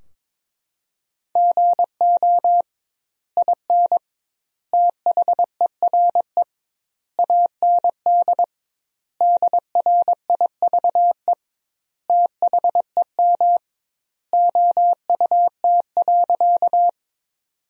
Morse code, heard as a keyed tone, about 22 wpm, 700 Hz. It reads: GO IN THERE AND DRIVE THEM OUT.
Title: 2x Word-Spacing - Sentences from Top 500 Words - 22wpm